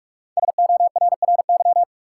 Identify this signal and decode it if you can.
Morse code, heard as a keyed tone, about 44 wpm, 700 Hz. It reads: SORRY
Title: Five Letter Words - 45wpm